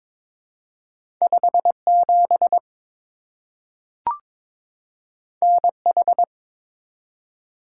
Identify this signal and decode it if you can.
Morse code, heard as a keyed tone, about 22 wpm, 700 Hz. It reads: 57 NH
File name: practice / 22wpm / Two Letter-Number Combinations - 22wpm